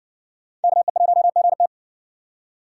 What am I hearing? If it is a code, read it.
Morse code, heard as a keyed tone, about 59 wpm, 700 Hz. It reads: KE1GT